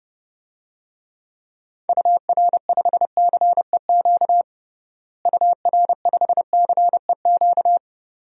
Morse code, heard as a keyed tone, about 30 wpm, 700 Hz. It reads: UR5CEQ UR5CEQ
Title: DXCC-100 Call Signs - 6 Characters - Repeated 2x - 30wpm